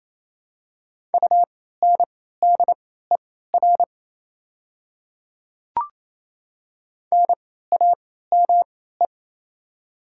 Morse code, heard as a keyed tone, about 28 wpm, spaced as for 16 wpm, 700 Hz. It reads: UNDER NAME
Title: 3x Character Spacing - Top 300 Words - 28wpm